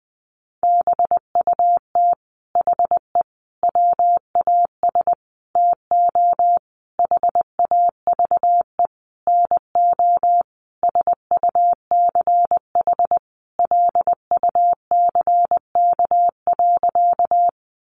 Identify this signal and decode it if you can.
Morse code, heard as a keyed tone, about 20 wpm, 700 Hz. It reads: BUT HE WAS TO HAVE NO SUCH LUCK.
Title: Sentences from Top 1500 Words 20wpm